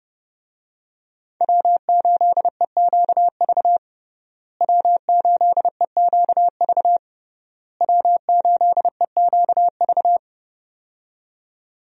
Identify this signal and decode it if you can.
Morse code, heard as a keyed tone, about 32 wpm, 700 Hz. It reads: W8EQV W8EQV W8EQV
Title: USA Call Signs - 5 Characters - Repeated 3x - 30wpm